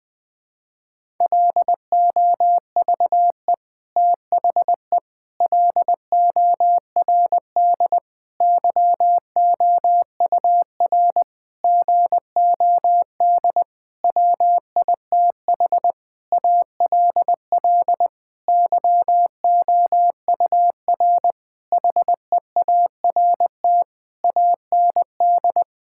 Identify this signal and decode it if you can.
Morse code, heard as a keyed tone, about 21 wpm, 700 Hz. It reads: LOVE THE LORD YOUR GOD WITH ALL YOUR HEART AND
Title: Sentences with 9-Word Phrases - 20wpm